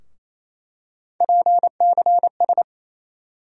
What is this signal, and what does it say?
Morse code, heard as a keyed tone, about 28 wpm, 700 Hz. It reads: PCS
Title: USA Call Sign Postfixes - 3 Letters 28wpm